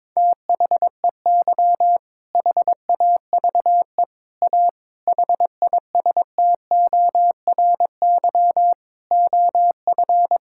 Morse code, heard as a keyed tone, about 22 wpm, 700 Hz. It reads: THEY HAVE A HISTORY OF
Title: Sentences from Top 1000 Words - Review Part 6 - 22wpm